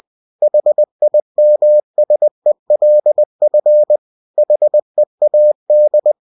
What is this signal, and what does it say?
Morse code, heard as a keyed tone, about 20 wpm, 600 Hz. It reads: HIMSELF HEAD